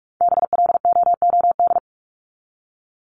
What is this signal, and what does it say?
Morse code, heard as a keyed tone, about 44 wpm, 700 Hz. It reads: BLOOD
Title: Top 1001-1250 Words 45wpm